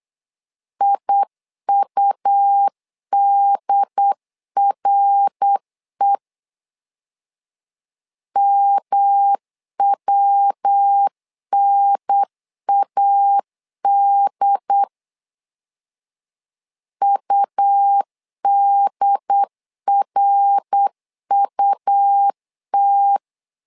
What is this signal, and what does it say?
Morse code, heard as a keyed tone, about 8 wpm, 800 Hz. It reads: IUDRE MWNAD UDRUT
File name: LEZIONI / 3 MWRO / 3 MWR 8wpm 2444